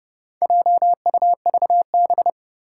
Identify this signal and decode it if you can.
Morse code, heard as a keyed tone, about 30 wpm, 690 Hz. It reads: JUVB
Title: Four Challenging Letter-Number Combinations 30wpm